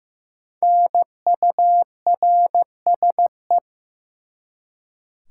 Morse code, heard as a keyed tone, about 15 wpm, 700 Hz. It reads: NURSE
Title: Five Letter Words - 15wpm